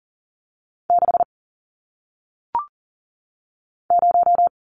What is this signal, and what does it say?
Morse code, heard as a keyed tone, about 40 wpm, 700 Hz. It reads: 6 0
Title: Numbers - 1-Digit 40wpm